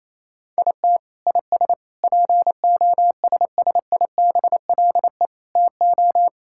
Morse code, heard as a keyed tone, about 27 wpm, 700 Hz. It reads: IT IS POSSIBLE TO
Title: Sentences from Top 1600 Words 28wpm